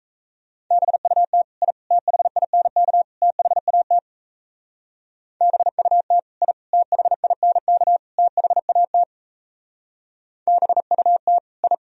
Morse code, heard as a keyed tone, about 34 wpm, 700 Hz. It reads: BUT I THINK THAT BUT I THINK THAT BUT I
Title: Speed-Racing - Sets of 4 Words 28wpm